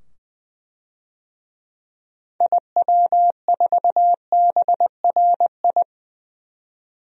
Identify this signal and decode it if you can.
Morse code, heard as a keyed tone, about 20 wpm, 700 Hz. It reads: IW4BRI
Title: DXCC-100 Call Signs - 6 Characters - 20wpm